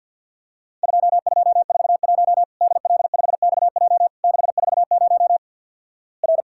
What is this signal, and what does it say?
Morse code, heard as a keyed tone, about 53 wpm, 700 Hz. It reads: 2241 DL5XJ 640 R
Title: Worked All Europe DX Contest - QTC Exchange with Repeats - 50wpm